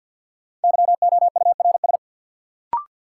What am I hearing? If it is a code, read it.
Morse code, heard as a keyed tone, about 53 wpm, 700 Hz. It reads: YOURS
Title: Sentences from Top 1500 Words - New Word Drill Alt - 50wpm